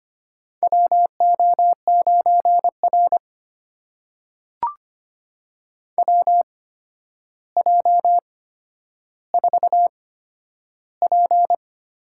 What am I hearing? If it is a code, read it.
Morse code, heard as a keyed tone, about 25 wpm, spaced as for 11 wpm, 700 Hz. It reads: WO9R WJ4P
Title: USA Call Signs - 4 Characters - 8x Character Spacing - 25wpm